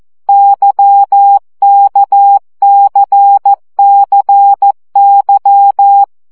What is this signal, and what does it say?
Morse code, heard as a keyed tone, about 14 wpm, 800 Hz. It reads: YKCCY
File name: CW gyakorlas k c y 12 baud